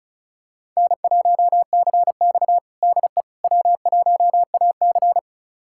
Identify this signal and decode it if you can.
Morse code, heard as a keyed tone, about 36 wpm, 700 Hz. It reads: N1CX DE W1AC
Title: USA Callsign DE Callsign - 4 Characters - 35wpm